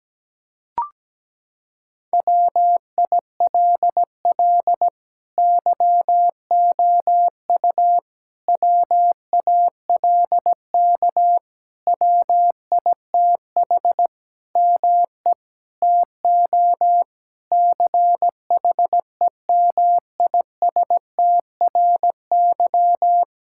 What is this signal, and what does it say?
Morse code, heard as a keyed tone, about 17 wpm, 700 Hz. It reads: WILL YOU WALK WITH ME TO CHEMISTRY
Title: Sentences from Top 1000 Words - Review Part 6 - 17wpm